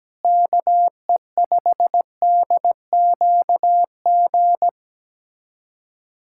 Morse code, heard as a keyed tone, about 17 wpm, 700 Hz. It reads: KE5DQG
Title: USA Callsign DE Callsign 17wpm